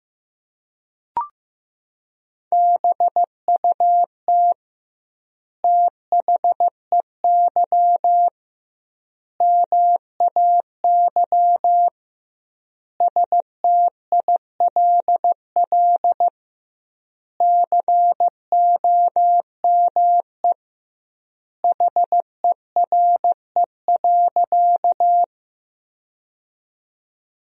Morse code, heard as a keyed tone, about 15 wpm, 700 Hz. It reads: BUT THEY MAY STILL COME HERE.
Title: 2x Word-Spacing - Sentences from Top 300 Words - 15wpm